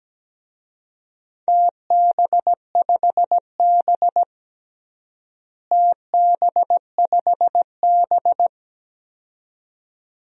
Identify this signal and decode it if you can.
Morse code, heard as a keyed tone, about 17 wpm, 700 Hz. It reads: TB5B TB5B
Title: DXCC-100 Call Signs - 4 Characters - Repeated 2x - 17wpm